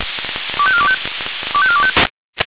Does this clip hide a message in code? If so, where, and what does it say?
Yes: five-tone selective call (ZVEI) from 0.6 to 1.9 s, heard as quick runs of five tones, tones of about 70 ms. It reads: 25425 25425